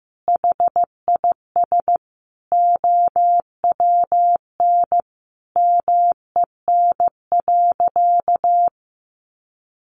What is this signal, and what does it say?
Morse code, heard as a keyed tone, about 15 wpm, 700 Hz. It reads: HIS OWN MEN.
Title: Sentences from Top 300 Words 15wpm